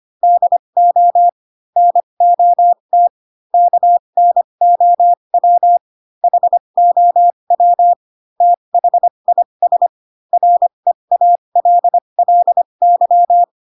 Morse code, heard as a keyed tone, about 25 wpm, spaced as for 22 wpm, 700 Hz. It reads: DO NOT KNOW HOW THIS REALLY